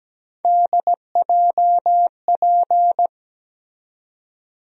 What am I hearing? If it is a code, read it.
Morse code, heard as a keyed tone, about 17 wpm, 700 Hz. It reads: DJP
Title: USA Call Sign Postfixes - 3 Letters 17wpm